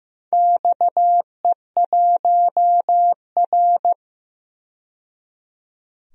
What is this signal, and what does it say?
Morse code, heard as a keyed tone, about 15 wpm, 700 Hz. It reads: XE1R
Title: DXCC-100 Call Signs - 4 Characters - 15wpm